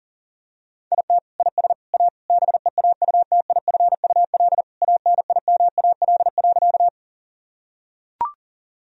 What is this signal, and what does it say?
Morse code, heard as a keyed tone, about 40 wpm, 700 Hz. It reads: IT IS A BEAUTIFUL ANIMAL.
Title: Sentences from Top 1000 Words - Review Part 1 - 40wpm